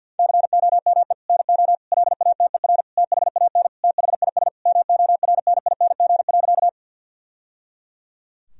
Morse code, heard as a keyed tone, about 50 wpm, 700 Hz. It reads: COME NO LATER THAN THIS MORNING.
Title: Sentences from Top 1000 Words - Review Part 4 - 50wpm